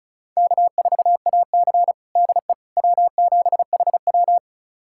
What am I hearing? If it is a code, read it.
Morse code, heard as a keyed tone, about 36 wpm, 700 Hz. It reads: K4AC DE W7HW
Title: USA Callsign DE Callsign - 4 Characters - 35wpm